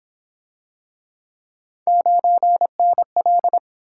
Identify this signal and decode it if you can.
Morse code, heard as a keyed tone, about 28 wpm, 700 Hz. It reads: 9NL